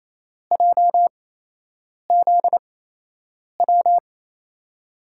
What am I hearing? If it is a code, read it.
Morse code, heard as a keyed tone, about 28 wpm, spaced as for 8 wpm, 700 Hz. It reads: JZW
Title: USA Call Signs - 6 Characters - 8x Character Spacing - 28wpm